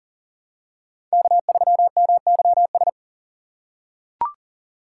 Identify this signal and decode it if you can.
Morse code, heard as a keyed tone, about 40 wpm, 700 Hz. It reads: K3MYS
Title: USA Call Signs - 5 Characters - 3x Character Spacing - 40wpm